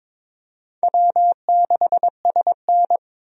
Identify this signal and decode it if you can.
Morse code, heard as a keyed tone, about 23 wpm, 700 Hz. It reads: W6SN
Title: USA Call Signs - 4 Characters - 22wpm